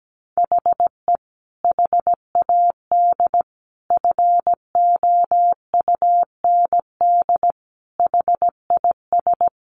Morse code, heard as a keyed tone, about 17 wpm, 700 Hz. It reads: HE HAD FOUND HIS
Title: A Sound Of Thunder 17wpm